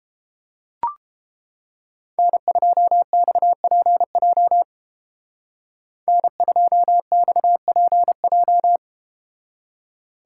Morse code, heard as a keyed tone, about 32 wpm, 700 Hz. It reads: N2XPJ N2XPJ